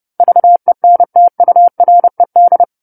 Morse code, heard as a keyed tone, about 30 wpm, 700 Hz. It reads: VENTURED
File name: 10-Chapter-10 30 WPM